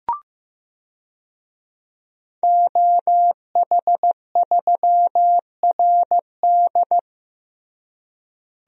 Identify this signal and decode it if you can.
Morse code, heard as a keyed tone, about 15 wpm, 700 Hz. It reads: OH3RD
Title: DXCC-100 Call Signs - 5 Characters - Repeated 3x - 15wpm